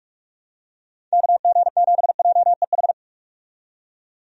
Morse code, heard as a keyed tone, about 46 wpm, 700 Hz. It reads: KG7JEH
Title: USA Call Signs - 6 Characters - 45wpm